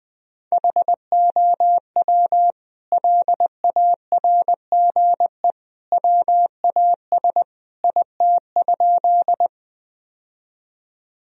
Morse code, heard as a keyed tone, about 21 wpm, 700 Hz. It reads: HOW LARGE WAS IT?
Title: Sentences from Top 1000 Words - Review Part 1 - 20wpm